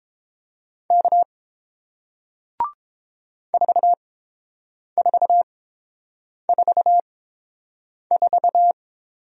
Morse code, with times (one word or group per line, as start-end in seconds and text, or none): 0.9-1.2 K
3.5-8.7 4444